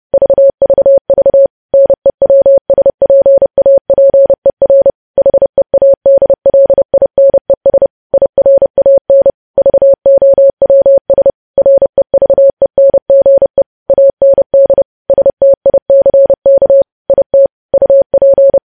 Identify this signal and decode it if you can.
Morse code, heard as a keyed tone, about 30 wpm, 550 Hz. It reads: VVV NEWSPAPER HEADLINES IRAN VOWS REVENGE AND STICK IT UP
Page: news headlines in morse code at 30 wpm